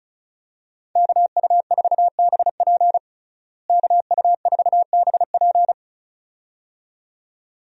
Morse code, heard as a keyed tone, about 36 wpm, 700 Hz. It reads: KU4BP KU4BP
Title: USA Call Signs - 5 Characters - Repeated 2x - 35wpm